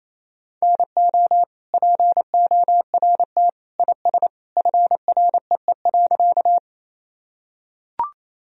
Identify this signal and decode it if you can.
Morse code, heard as a keyed tone, about 28 wpm, 700 Hz. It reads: NO PORT IS FREE.
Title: Sentences from Top 1000 Words - Review Part 1 - 28wpm